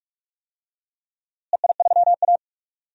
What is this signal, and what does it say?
Morse code, heard as a keyed tone, about 46 wpm, 700 Hz. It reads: EI3A
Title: DXCC-100 Call Signs - 4 Characters - Repeated 3x - 45wpm